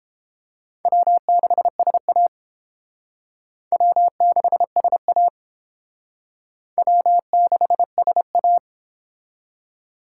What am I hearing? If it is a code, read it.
Morse code, heard as a keyed tone, about 31 wpm, 700 Hz. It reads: W6SA W6SA W6SA